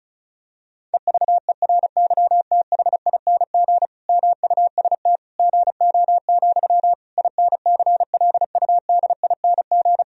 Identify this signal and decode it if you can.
Morse code, heard as a keyed tone, about 36 wpm, 700 Hz. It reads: EVERYTHING MUST GO, INCLUDING
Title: Sentences from Top 1000 Words - Review Part 4 - 35wpm